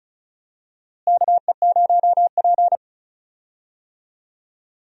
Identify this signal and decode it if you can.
Morse code, heard as a keyed tone, about 34 wpm, 700 Hz. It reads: KE0P
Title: USA Call Signs - 4 Characters - Repeated 2x - 35wpm